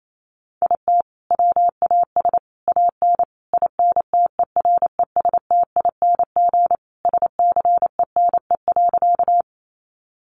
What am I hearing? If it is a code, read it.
Morse code, heard as a keyed tone, about 28 wpm, 700 Hz. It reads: IT WAS AN INTERESTING SCENE.
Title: Sentences from Top 1100 Words 28wpm